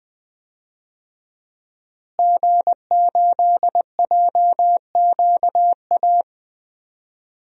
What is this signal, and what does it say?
Morse code, heard as a keyed tone, about 20 wpm, 700 Hz. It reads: G8JQA
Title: DXCC-100 Call Signs - 5 Characters - Repeated 3x - 20wpm